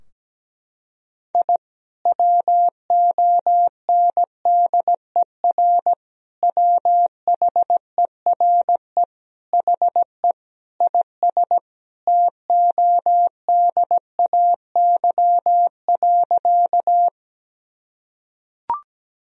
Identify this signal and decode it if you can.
Morse code, heard as a keyed tone, about 17 wpm, 700 Hz. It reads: I WONDER WHERE HE IS TODAY.
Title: Sentences from Top 1000 Words - Review Part 3 - 17wpm